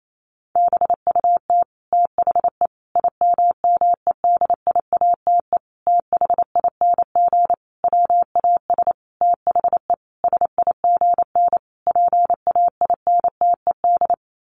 Morse code, heard as a keyed tone, about 28 wpm, 700 Hz. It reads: BUT THE IMMEDIATE THING WAS THE SIGN PAINTED
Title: A Sound Of Thunder 28wpm